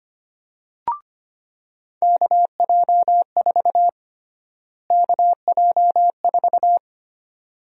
Morse code, heard as a keyed tone, about 25 wpm, 700 Hz. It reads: KJ4 KJ4
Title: USA Call Sign Prefixes - Repeated 2x - 25wpm